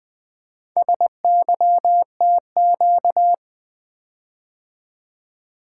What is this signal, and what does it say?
Morse code, heard as a keyed tone, about 20 wpm, 700 Hz. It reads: SYTQ
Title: Four Letter-Number Combinations - 20wpm